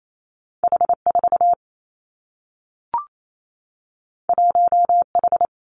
Morse code, heard as a keyed tone, about 28 wpm, 700 Hz. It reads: H 4 1 H
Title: Two Challenging Letter-Number Combinations 28wpm